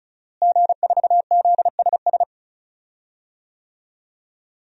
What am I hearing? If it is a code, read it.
Morse code, heard as a keyed tone, about 36 wpm, 700 Hz. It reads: G4ZSS
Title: DXCC-100 Call Signs - 5 Characters - Repeated 2x - 35wpm